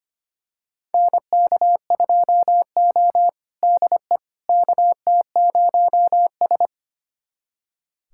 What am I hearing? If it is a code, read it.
Morse code, heard as a keyed tone, about 25 wpm, 700 Hz. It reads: NK2O DE KT0S
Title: USA Callsign DE Callsign - 4 Characters - 25wpm